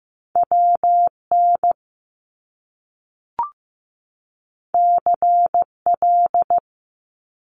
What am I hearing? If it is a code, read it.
Morse code, heard as a keyed tone, about 15 wpm, 700 Hz. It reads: WN CL
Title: Top Two Letter Combinations in Words 15wpm